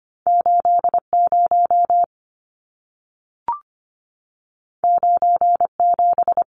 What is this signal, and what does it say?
Morse code, heard as a keyed tone, about 25 wpm, 700 Hz. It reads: 80 97
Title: Numbers - 2-Digit 25wpm